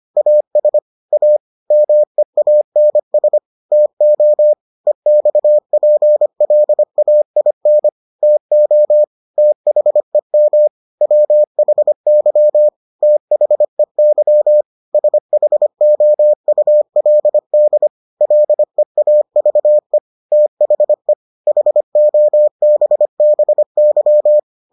Morse code, heard as a keyed tone, about 25 wpm, 600 Hz. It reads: AS A MEANS TO EXPLAIN TO THEM WHY THEY SHOULD LEAVE THE HOBBY